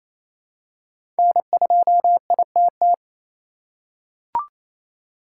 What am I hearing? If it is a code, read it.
Morse code, heard as a keyed tone, about 28 wpm, 700 Hz. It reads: N2ITT
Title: USA Call Signs - 5 Characters - 8x Character Spacing - 28wpm